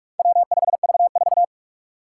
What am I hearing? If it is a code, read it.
Morse code, heard as a keyed tone, about 44 wpm, 700 Hz. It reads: W5V4
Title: Four Challenging Letter-Number Combinations 45wpm